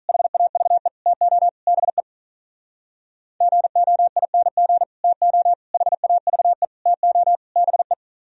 Morse code, heard as a keyed tone, about 44 wpm, 700 Hz. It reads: HAVE TO BE GOING TO HAVE TO BE